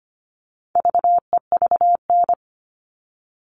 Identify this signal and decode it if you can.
Morse code, heard as a keyed tone, about 25 wpm, 700 Hz. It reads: VEVN
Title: All Cut Numbers - 4-Digit 25wpm